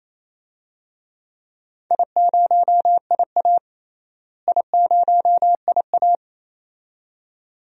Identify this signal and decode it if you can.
Morse code, heard as a keyed tone, about 28 wpm, 700 Hz. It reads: I0IA I0IA
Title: DXCC-100 Call Signs - 4 Characters - Repeated 2x - 28wpm